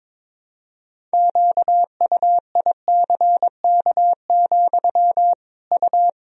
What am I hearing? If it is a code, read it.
Morse code, heard as a keyed tone, about 22 wpm, 700 Hz. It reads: QUICK, U
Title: Sentences from Top 500 Words and Abbreviations 22wpm